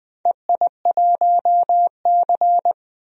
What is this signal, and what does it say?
Morse code, heard as a keyed tone, about 20 wpm, 700 Hz. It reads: EI1C
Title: DXCC-100 Call Signs - 4 Characters - 20wpm